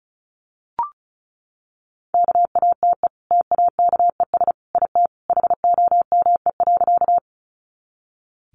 Morse code, heard as a keyed tone, about 34 wpm, 700 Hz. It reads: KATE TAKES IT HOME.
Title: Easy Phrases 35wpm